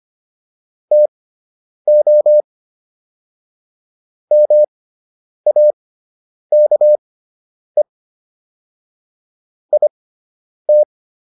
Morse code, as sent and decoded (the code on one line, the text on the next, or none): - --- / -- .- -.- . / .. -
TO MAKE IT